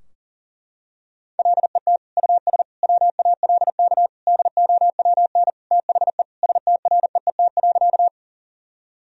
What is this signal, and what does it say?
Morse code, heard as a keyed tone, about 40 wpm, 700 Hz. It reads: LET US WALK DOWN THE STREET.
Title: Sentences from Top 1000 Words - Review Part 3 - 40wpm